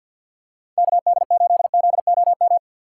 Morse code, heard as a keyed tone, about 50 wpm, 700 Hz. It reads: KD8ZOM
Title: USA Call Signs - 6 Characters - 4x Character Spacing - 50wpm